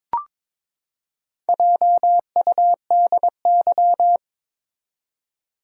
Morse code, heard as a keyed tone, about 22 wpm, 700 Hz. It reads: JUDY
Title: USA Common Names - Max 4 Letters - 22wpm